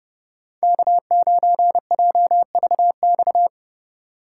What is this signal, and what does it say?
Morse code, heard as a keyed tone, about 31 wpm, 700 Hz. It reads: K9JVX